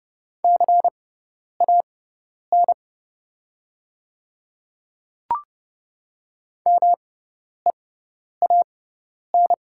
Morse code, heard as a keyed tone, about 30 wpm, spaced as for 10 wpm, 690 Hz. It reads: CAN MEAN
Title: 6x Character Spacing - Top 300 Words - 30wpm